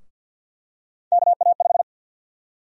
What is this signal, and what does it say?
Morse code, heard as a keyed tone, about 51 wpm, 700 Hz. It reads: KA5
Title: USA Call Sign Prefixes - Repeated 2x - 50wpm